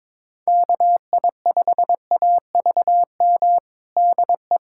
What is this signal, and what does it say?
Morse code, heard as a keyed tone, about 22 wpm, 700 Hz. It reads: KI5AVM DE
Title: USA Callsign DE Callsign - 6 Characters - 22wpm